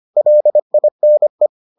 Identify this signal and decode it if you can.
Morse code, heard as a keyed tone, about 25 wpm, 600 Hz. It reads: LINE